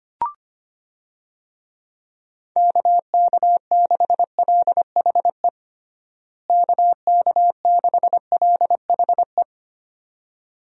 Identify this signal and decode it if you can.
Morse code, heard as a keyed tone, about 25 wpm, 700 Hz. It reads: KK6LHE KK6LHE
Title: USA Call Signs - 6 Characters - Repeated 2x - 25wpm